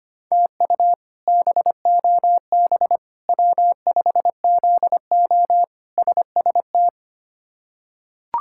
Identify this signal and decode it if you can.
Morse code, heard as a keyed tone, about 25 wpm, 700 Hz. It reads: TU BOB W5ZO SST
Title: K1USN SST QSO Practice - 25wpm